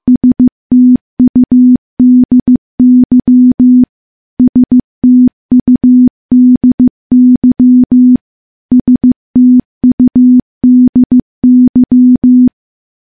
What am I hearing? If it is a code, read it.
Morse code, heard as a keyed tone, about 15 wpm, 250 Hz. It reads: STUDY STUDY STUDY
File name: morse / morse-training-15WPM